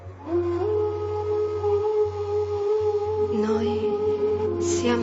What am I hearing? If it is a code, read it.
Congestion tone (425 Hz).